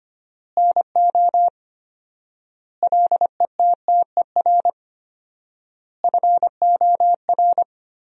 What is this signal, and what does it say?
Morse code, heard as a keyed tone, about 25 wpm, 700 Hz. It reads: NO LETTER FOR
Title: 4x Word-Spacing - Sentences from Top 300 Words - 25wpm